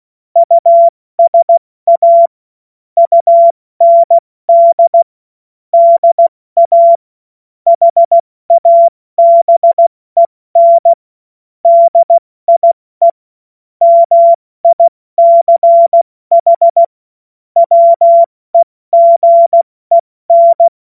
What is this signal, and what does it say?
Morse code, heard as a keyed tone, about 16 wpm, spaced as for 14 wpm, 680 Hz. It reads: USA UND DA HABEN DIE MICH WEGEN
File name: Morsekurs / Witze / Witz 26. April 2022 70 F80